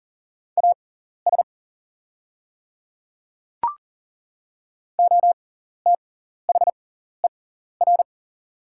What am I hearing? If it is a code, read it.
Morse code, heard as a keyed tone, about 40 wpm, spaced as for 14 wpm, 700 Hz. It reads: AS OTHER
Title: 6x Character Spacing - Top 100 Words - 40wpm